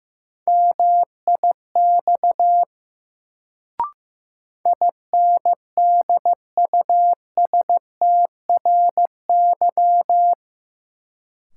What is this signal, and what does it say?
Morse code, heard as a keyed tone, about 16 wpm, 700 Hz. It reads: MIX INDUSTRY
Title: Sentences from Top 1000 Words - Review Part 4 - New Word Drill - 15wpm